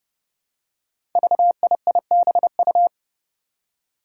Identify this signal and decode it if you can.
Morse code, heard as a keyed tone, about 29 wpm, 700 Hz. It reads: VIIBU
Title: Five Challenging Letter-Number Combinations 30wpm